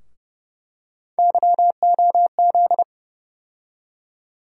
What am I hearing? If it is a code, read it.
Morse code, heard as a keyed tone, about 30 wpm, 700 Hz. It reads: YOZ
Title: USA Call Sign Postfixes - 3 Letters 30wpm